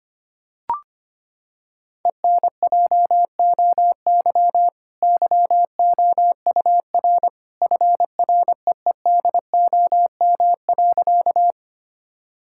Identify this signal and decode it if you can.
Morse code, heard as a keyed tone, about 25 wpm, 700 Hz. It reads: ENJOY YOUR FREEDOM.
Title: Sentences from Top 1000 Words - Review Part 5 - 25wpm